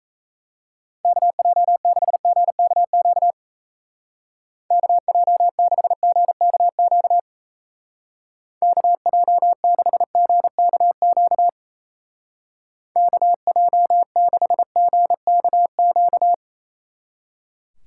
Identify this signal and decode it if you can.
Morse code, heard as a keyed tone, about 34 wpm, 700 Hz. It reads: KJ6GKQ KJ6GKQ KJ6GKQ KJ6GKQ
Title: Speed Racing - USA 2x3 Call Signs 28wpm